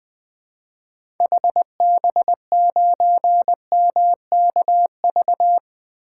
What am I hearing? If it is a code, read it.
Morse code, heard as a keyed tone, about 21 wpm, 700 Hz. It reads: HB9MKV
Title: 2023 SOTA Chaser Call Signs - 20wpm